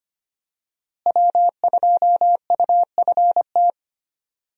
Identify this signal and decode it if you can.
Morse code, heard as a keyed tone, about 25 wpm, 700 Hz. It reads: W2UFT
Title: USA Call Signs - 5 Characters - Repeated 3x - 25wpm